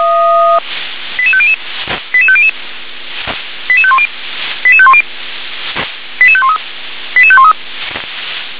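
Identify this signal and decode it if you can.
Five-tone selective call (ZVEI), heard as quick runs of five tones, tones of about 70 ms. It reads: 88500 88500 88510 88510 88513 88513
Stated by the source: ZVEI five-tone selective call received on SDR.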